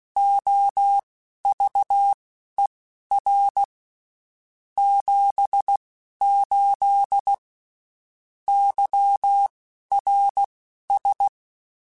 Morse code, heard as a keyed tone, about 16 wpm, spaced as for 12 wpm, 800 Hz. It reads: OVER 78 YRS